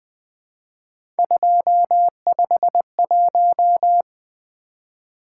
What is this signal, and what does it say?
Morse code, heard as a keyed tone, about 20 wpm, 700 Hz. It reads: 251
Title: Three Letter-Number Combinations - 20wpm